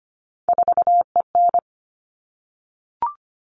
Morse code, heard as a keyed tone, about 25 wpm, 700 Hz. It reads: 4EN
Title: All Cut Numbers - 3-Digit 25wpm